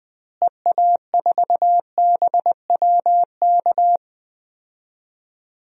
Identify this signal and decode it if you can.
Morse code, heard as a keyed tone, about 20 wpm, 700 Hz. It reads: EA4BWK
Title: DXCC-100 Call Signs - 6 Characters - Repeated 2x - 20wpm